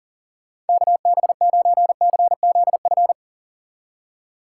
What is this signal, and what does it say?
Morse code, heard as a keyed tone, about 52 wpm, 700 Hz. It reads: KB9CZF